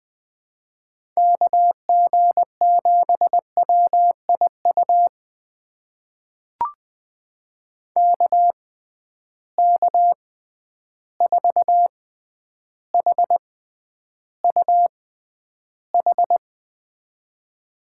Morse code, heard as a keyed tone, about 20 wpm, spaced as for 10 wpm, 700 Hz. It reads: KG7WIU KK4HUH
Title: USA Call Signs - 6 Characters - 6x Character Spacing - 20wpm